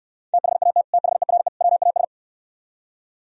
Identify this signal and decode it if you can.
Morse code, heard as a keyed tone, about 76 wpm, 700 Hz. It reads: THAT THERE WAS